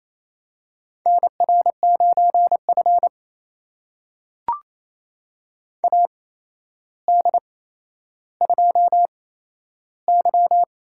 Morse code, heard as a keyed tone, about 28 wpm, spaced as for 12 wpm, 700 Hz. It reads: NR9F AD2Y
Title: USA Call Signs - 4 Characters - 8x Character Spacing - 28wpm